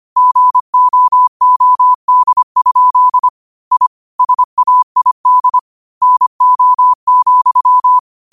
Morse code, heard as a keyed tone, about 25 wpm, 1000 Hz. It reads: GOOD? I SAID NO,
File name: qotdfiles / 25wpm